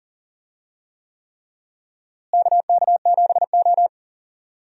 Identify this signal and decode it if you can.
Morse code, heard as a keyed tone, about 40 wpm, 700 Hz. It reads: KK7O
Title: USA Call Signs - 4 Characters - Repeated 3x - 40wpm